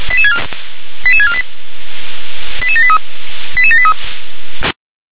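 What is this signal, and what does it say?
Five-tone selective call (ZVEI), heard as quick runs of five tones, tones of about 70 ms. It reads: 88648 88783 88783
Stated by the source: ZVEI five-tone selective call received on an SDR.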